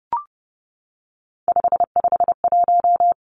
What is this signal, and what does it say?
Morse code, heard as a keyed tone, about 30 wpm, 700 Hz. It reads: 551
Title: Three Challenging Letter-Number Combinations 30wpm